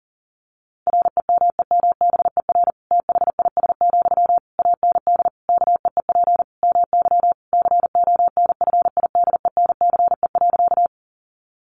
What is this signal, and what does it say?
Morse code, heard as a keyed tone, about 40 wpm, 700 Hz. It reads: REMEMBER THIS, AND KEEP MY CONFIDENCE.
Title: Sentences from Top 1500 Words 40wpm